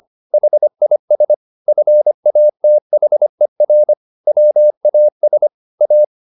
Morse code, heard as a keyed tone, about 25 wpm, 600 Hz. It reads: HIS FATHER WAS A
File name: the old man and the sea part 1 25wpm